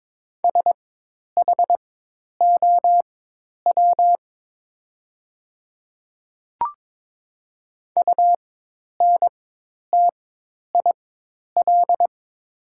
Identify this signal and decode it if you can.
Morse code, heard as a keyed tone, about 22 wpm, spaced as for 10 wpm, 700 Hz. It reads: SHOW UNTIL
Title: 4x Character Spacing - Top 300 Words - 22wpm